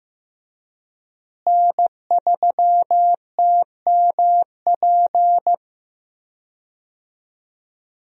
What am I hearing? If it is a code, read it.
Morse code, heard as a keyed tone, about 15 wpm, 700 Hz. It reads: N3TMP
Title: USA Call Signs - 5 Characters - Repeated 2x - 15wpm